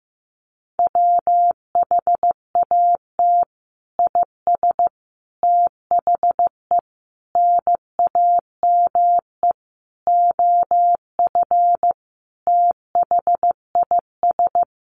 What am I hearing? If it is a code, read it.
Morse code, heard as a keyed tone, about 15 wpm, 700 Hz. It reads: WHAT IS THE NAME OF THIS
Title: Sentences from Top 1500 Words 15wpm